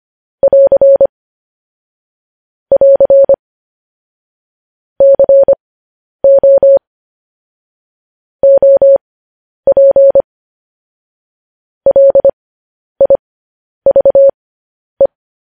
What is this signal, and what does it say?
Morse code, heard as a keyed tone, about 25 wpm, spaced as for 10 wpm, 550 Hz. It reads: <AR> <AR> CO OP LIVE